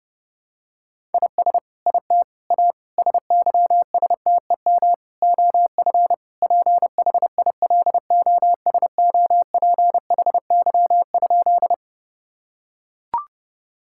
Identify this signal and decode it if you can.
Morse code, heard as a keyed tone, about 30 wpm, 700 Hz. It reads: IS IT A SYSTEM OF PHILOSOPHY?
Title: Sentences from Top 1000 Words - Review Part 6 - 30wpm